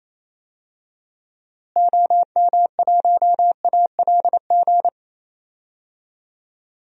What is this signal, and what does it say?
Morse code, heard as a keyed tone, about 28 wpm, 700 Hz. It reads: OM1ALG
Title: DXCC-100 Call Signs - 6 Characters - Repeated 2x - 28wpm